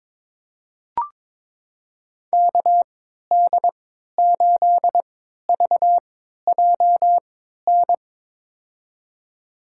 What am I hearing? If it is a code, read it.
Morse code, heard as a keyed tone, about 22 wpm, spaced as for 13 wpm, 700 Hz. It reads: KD8VJN
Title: USA Call Signs - 6 Characters - 3x Character Spacing - 22wpm